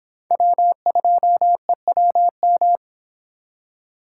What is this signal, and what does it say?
Morse code, heard as a keyed tone, about 25 wpm, 700 Hz. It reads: W2EWM